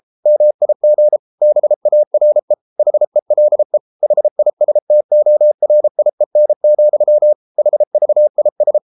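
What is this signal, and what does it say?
Morse code, heard as a keyed tone, about 33 wpm, 600 Hz. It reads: MIG BARE HELE HISTORIEN, HVIS
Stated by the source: Morse code at 33 wpm, QRQ.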